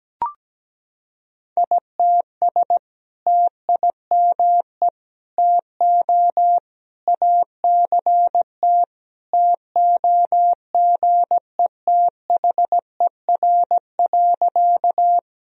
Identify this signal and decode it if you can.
Morse code, heard as a keyed tone, about 17 wpm, 700 Hz. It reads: ITS TIME TO ACT TOGETHER.
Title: Sentences from Top 1000 Words - Review Part 1 - 17wpm